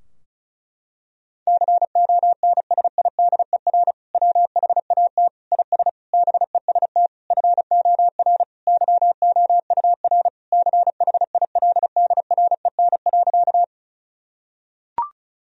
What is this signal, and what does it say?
Morse code, heard as a keyed tone, about 36 wpm, 700 Hz. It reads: CONSIDER WHAT IS BEST FOR YOUR CHILDREN.
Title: Sentences from Top 1000 Words - Review Part 4 - 35wpm